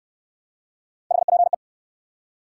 Morse code, heard as a keyed tone, about 69 wpm, 700 Hz. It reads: HPE